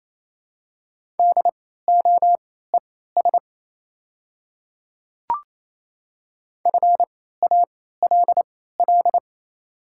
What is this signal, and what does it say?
Morse code, heard as a keyed tone, about 28 wpm, spaced as for 16 wpm, 700 Hz. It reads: DOES FALL
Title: 3x Character Spacing - Top 500 Words - 28wpm